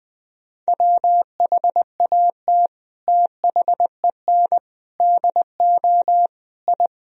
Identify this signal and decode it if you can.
Morse code, heard as a keyed tone, about 20 wpm, 700 Hz. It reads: WHAT THEN DO I
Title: Sentences from Top 1000 Words - Review Part 2 - 20wpm